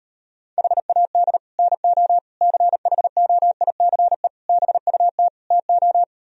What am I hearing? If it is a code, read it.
Morse code, heard as a keyed tone, about 38 wpm, 700 Hz. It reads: HAD NO CHOICE BUT TO